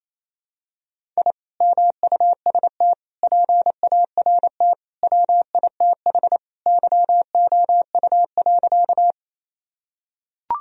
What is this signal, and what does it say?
Morse code, heard as a keyed tone, about 28 wpm, 700 Hz. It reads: I MUST PART WITH YOU.
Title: Sentences from Top 1000 Words - Review Part 1 - 28wpm